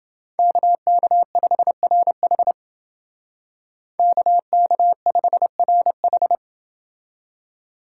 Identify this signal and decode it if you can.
Morse code, heard as a keyed tone, about 29 wpm, 700 Hz. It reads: KK5RH KK5RH